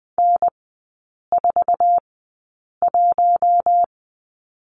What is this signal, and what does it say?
Morse code, heard as a keyed tone, about 20 wpm, spaced as for 8 wpm, 700 Hz. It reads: N41
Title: 053 Lesson 10e - Letter r - Three Characters 20wpm